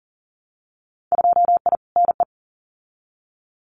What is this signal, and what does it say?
Morse code, heard as a keyed tone, about 41 wpm, 700 Hz. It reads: 2I NE
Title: Winter Field Day Exchanges Encore 40wpm